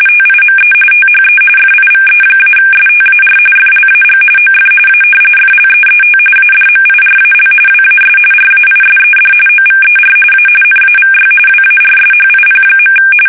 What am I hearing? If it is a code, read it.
RTTY, about 45.45 baud. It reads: TIME THE FIVE-STAR FLAG OF A FLEET ADMIRAL WAS RAISED OVER A NAVY WARSHIP.
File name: fsk850 / battleshipnj